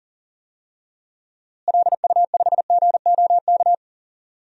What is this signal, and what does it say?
Morse code, heard as a keyed tone, about 37 wpm, 700 Hz. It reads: LU5GOK
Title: DXCC-100 Call Signs - 6 Characters - Repeated 3x - 40wpm